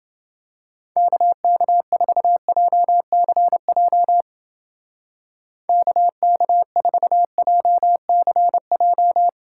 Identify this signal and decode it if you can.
Morse code, heard as a keyed tone, about 30 wpm, 700 Hz. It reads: KK4JCJ KK4JCJ